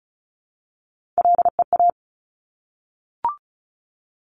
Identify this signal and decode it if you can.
Morse code, heard as a keyed tone, about 36 wpm, 700 Hz. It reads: LEA
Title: Top Three Letter Combinations in Words 35wpm